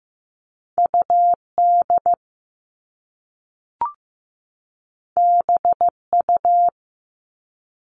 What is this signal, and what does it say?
Morse code, heard as a keyed tone, about 15 wpm, 700 Hz. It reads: UD BU
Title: All Cut Numbers - 2-Digit 15wpm